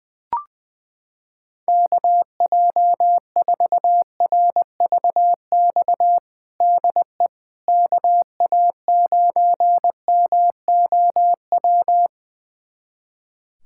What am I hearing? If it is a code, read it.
Morse code, heard as a keyed tone, about 20 wpm, 700 Hz. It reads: KJ4RVX DE KA9MOW
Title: USA Callsign DE Callsign - 6 Characters - 20wpm